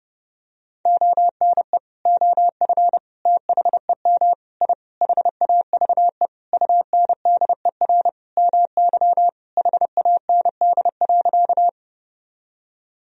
Morse code, heard as a keyed tone, about 30 wpm, 700 Hz. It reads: ONE OF THEM I HAVE UNDER MY HAND.
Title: Sentences from Top 1000 Words - Review Part 1 - 30wpm